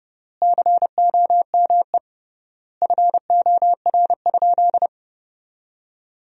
Morse code, heard as a keyed tone, about 30 wpm, 700 Hz. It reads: COME FOR?
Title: 3x Word-Spacing - Sentences from Top 100 Words - 30wpm